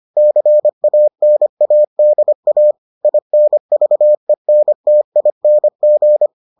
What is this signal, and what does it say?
Morse code, heard as a keyed tone, about 25 wpm, 600 Hz. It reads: CANADA INVENTING